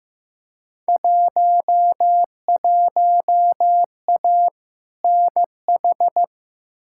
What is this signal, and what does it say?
Morse code, heard as a keyed tone, about 15 wpm, 700 Hz. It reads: 11A NH
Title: Field Day Exchanges Encore - 15wpm